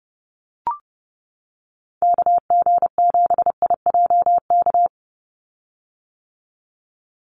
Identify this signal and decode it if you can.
Morse code, heard as a keyed tone, about 30 wpm, 700 Hz. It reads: KG7IJK
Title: USA Call Signs 30wpm